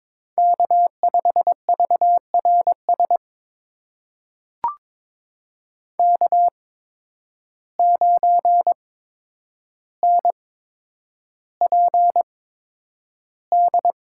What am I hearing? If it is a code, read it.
Morse code, heard as a keyed tone, about 21 wpm, spaced as for 9 wpm, 700 Hz. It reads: K5VRS K9NPD
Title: USA Call Signs - 5 Characters - 8x Character Spacing - 22wpm